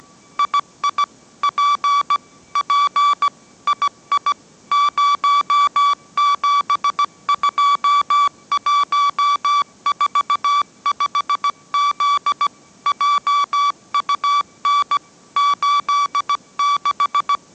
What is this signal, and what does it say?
Morse code, heard as a keyed tone, about 17 wpm, 1150 Hz. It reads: II P P II 072145Z JUN 86
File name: mrx / mrx080 18wpm